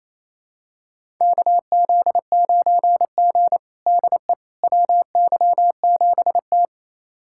Morse code, heard as a keyed tone, about 28 wpm, 700 Hz. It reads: KZ9G DE WY7T
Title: USA Callsign DE Callsign - 4 Characters - 28wpm